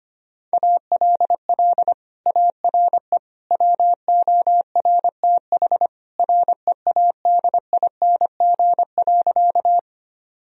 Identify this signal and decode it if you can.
Morse code, heard as a keyed tone, about 25 wpm, 700 Hz. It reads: ALL ARE WORTH READING.
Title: Sentences from Top 1000 Words - Review Part 3 - 25wpm